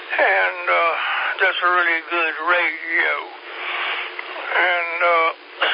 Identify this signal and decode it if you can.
Emergency Alert System (SAME) tones, heard as harsh buzzing bursts, with speech. It is heard from 1.2 to 4.8 s.